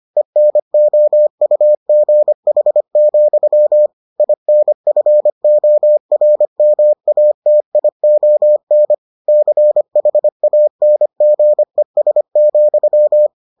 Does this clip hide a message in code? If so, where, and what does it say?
Yes: Morse code from 0.2 to 13.3 s, heard as a keyed tone, about 25 wpm, 600 Hz. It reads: ENOUGH, INFORMATION CHANGES,